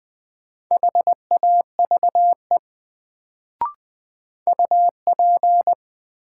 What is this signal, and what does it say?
Morse code, heard as a keyed tone, about 21 wpm, 700 Hz. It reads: HAVE UP
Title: Sets of 2 Words - New Word Drill - 20wpm